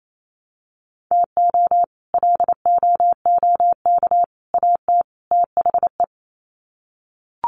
Morse code, heard as a keyed tone, about 28 wpm, 700 Hz. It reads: TO LOOK AT THE
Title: Sets of 4 Words 28wpm